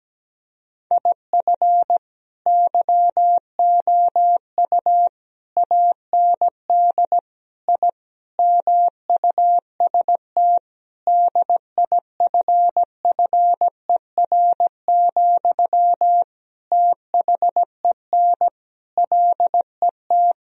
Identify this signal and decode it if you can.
Morse code, heard as a keyed tone, about 17 wpm, 700 Hz. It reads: IF YOU AND I MUST DIFFER, THEN LET
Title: Sentences from Top 1000 Words - Review Part 1 - 17wpm